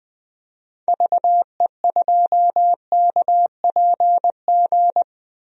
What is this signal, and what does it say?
Morse code, heard as a keyed tone, about 20 wpm, 700 Hz. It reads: VE2KPG
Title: DXCC-100 Call Signs - 6 Characters - 20wpm